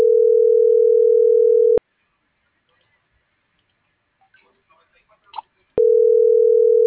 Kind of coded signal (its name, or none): telephone call progress tone